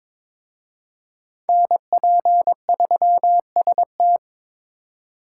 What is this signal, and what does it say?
Morse code, heard as a keyed tone, about 26 wpm, 700 Hz. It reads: NP3ST
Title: Speed Racing - USA 2x2 Call Signs 22wpm